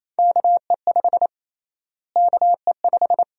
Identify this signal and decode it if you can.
Morse code, heard as a keyed tone, about 28 wpm, 700 Hz. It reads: KE5 KE5
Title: USA Call Sign Prefixes - Repeated 2x - 28wpm